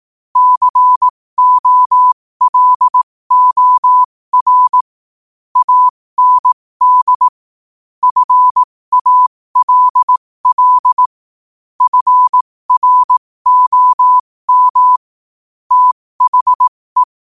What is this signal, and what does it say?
Morse code, heard as a keyed tone, about 18 wpm, spaced as for 15 wpm, 1000 Hz. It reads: COLOR AND FALL FROM THE
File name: qotdfiles / 15wpm